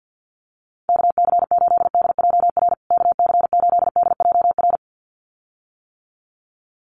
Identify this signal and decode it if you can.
Morse code, heard as a keyed tone, about 50 wpm, 700 Hz. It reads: KC8DJR KC8DJR
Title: USA Call Signs Repeated 2x 50wpm